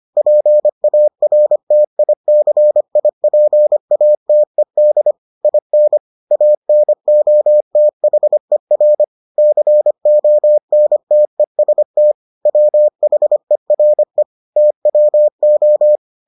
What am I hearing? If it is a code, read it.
Morse code, heard as a keyed tone, about 25 wpm, 600 Hz. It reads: PARTICIPATED IN ANOTHER CONTEST WHERE TWO